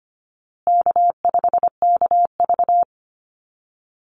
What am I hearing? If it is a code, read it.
Morse code, heard as a keyed tone, about 25 wpm, 700 Hz. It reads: K5KV
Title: Easy Call Signs Repeated 3x - 25wpm